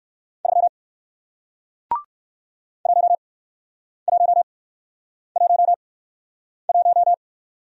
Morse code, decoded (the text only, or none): 3 1111